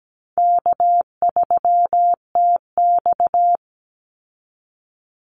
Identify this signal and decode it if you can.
Morse code, heard as a keyed tone, about 17 wpm, 700 Hz. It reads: K3TX
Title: Call Signs - 17wpm